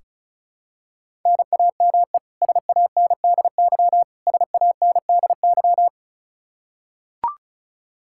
Morse code, heard as a keyed tone, about 36 wpm, 700 Hz. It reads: NAME SANDY SANDY
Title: K1USN SST QSO Practice 35wpm